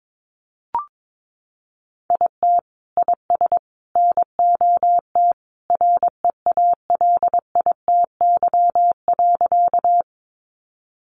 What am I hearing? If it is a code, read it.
Morse code, heard as a keyed tone, about 22 wpm, 700 Hz. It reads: IT IS NOT REALITY.
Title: Sentences from Top 1200 Words 22wpm